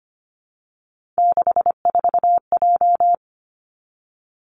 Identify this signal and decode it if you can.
Morse code, heard as a keyed tone, about 25 wpm, 700 Hz. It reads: 64J
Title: Three Challenging Letter-Number Combinations 25wpm